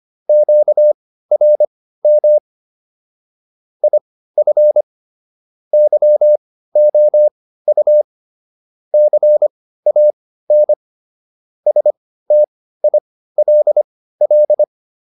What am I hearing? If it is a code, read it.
Morse code, heard as a keyed tone, about 25 wpm, spaced as for 15 wpm, 600 Hz. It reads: QRM IF YOU CAN STILL